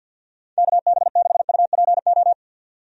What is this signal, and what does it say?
Morse code, heard as a keyed tone, about 51 wpm, 700 Hz. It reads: KB6UPO
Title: USA Call Signs - 6 Characters - 50wpm